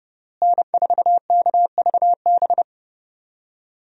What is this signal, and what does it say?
Morse code, heard as a keyed tone, about 31 wpm, 700 Hz. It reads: N4KVB
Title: USA Call Signs - 5 Characters - Repeated 3x - 30wpm